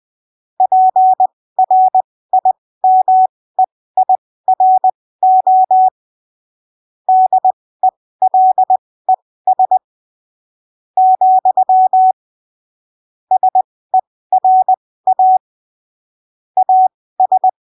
Morse code, heard as a keyed tone, about 20 wpm, spaced as for 14 wpm, 750 Hz. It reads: PRIMEIRO DELES, SERA AS
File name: arquivos / texto / texto 30 20ppm